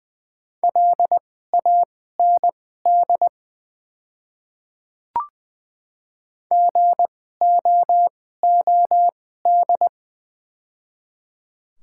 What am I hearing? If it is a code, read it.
Morse code, heard as a keyed tone, about 20 wpm, spaced as for 14 wpm, 700 Hz. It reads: LAND GOOD
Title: 2x Character Spacing - Top 200 Words - 20wpm